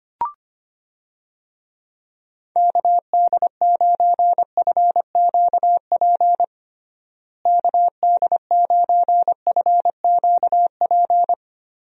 Morse code, heard as a keyed tone, about 25 wpm, 700 Hz. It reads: KD9FQP KD9FQP
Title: USA Call Signs - 6 Characters - Repeated 2x - 25wpm